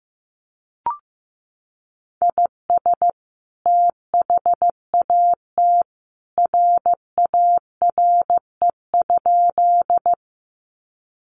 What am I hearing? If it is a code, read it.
Morse code, heard as a keyed tone, about 15 wpm, 700 Hz. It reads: IS THAT RARE?
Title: Sentences from Top 1100 Words 15wpm